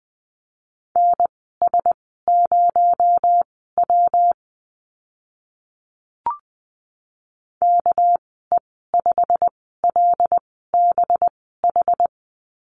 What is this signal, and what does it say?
Morse code, heard as a keyed tone, about 20 wpm, spaced as for 14 wpm, 700 Hz. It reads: NS0W KE5LBH
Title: 219 Lesson 35g - Number 0 - Call Signs 20wpm